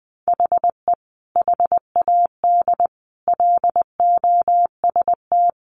Morse code, heard as a keyed tone, about 20 wpm, 700 Hz. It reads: HE HAD LOST
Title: Sentences from Top 1400 Words 20wpm